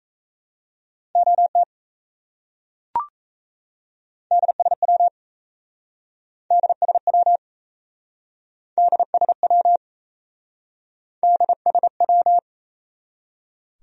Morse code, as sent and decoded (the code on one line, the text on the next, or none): --- - / -.. ... .-- / -.. ... .-- / -.. ... .-- / -.. ... .--
OT DSW DSW DSW DSW